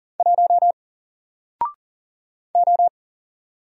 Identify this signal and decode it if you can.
Morse code, heard as a keyed tone, about 40 wpm, 700 Hz. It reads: 1 O
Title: Single Character (Letter:Number:Punctuation:Prosign) 40wpm